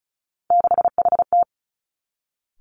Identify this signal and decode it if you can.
Morse code, heard as a keyed tone, about 36 wpm, 700 Hz. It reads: 6HT
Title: Three Challenging Letter-Number Combinations 35wpm